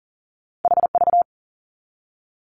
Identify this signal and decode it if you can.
Morse code, heard as a keyed tone, about 39 wpm, 700 Hz. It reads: HV
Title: Two Challenging Letter-Number Combinations 40wpm